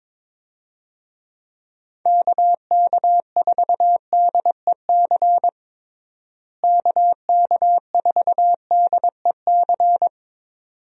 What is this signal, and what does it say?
Morse code, heard as a keyed tone, about 22 wpm, 700 Hz. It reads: KK4DEC KK4DEC
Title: USA Call Signs - 6 Characters - Repeated 2x - 22wpm